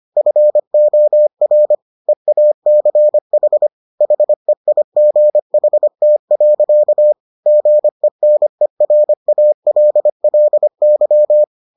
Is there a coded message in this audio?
Yes: Morse code from 0.2 to 11.4 s, heard as a keyed tone, about 25 wpm, 600 Hz. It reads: FOR EACH HEIGHT. GENERALLY